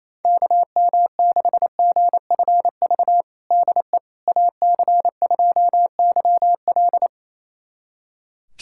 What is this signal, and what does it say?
Morse code, heard as a keyed tone, about 28 wpm, 700 Hz. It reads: KM6GFV DE AC2YL
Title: USA Callsign DE Callsign 28wpm